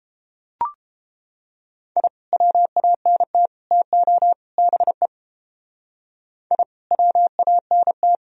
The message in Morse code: I WANT TO BE I WANT

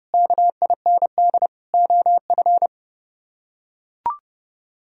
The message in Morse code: KIND OF

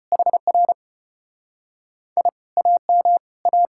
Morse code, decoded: HR I AM A

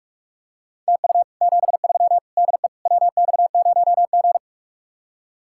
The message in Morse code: TU 73 DE WX0G

